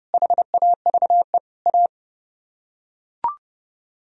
Morse code, heard as a keyed tone, about 28 wpm, 700 Hz. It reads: HAVE A